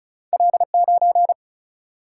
Morse code, heard as a keyed tone, about 36 wpm, 700 Hz. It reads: L9
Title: Single-Letter Single-Number 35wpm